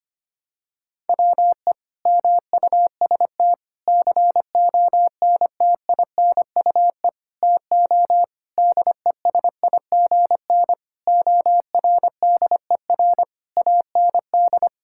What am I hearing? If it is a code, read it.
Morse code, heard as a keyed tone, about 25 wpm, 700 Hz. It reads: WE MUST CONTINUE TO DESIGN ORDER AND
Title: Sentences from Top 1000 Words - Review Part 6 - 25wpm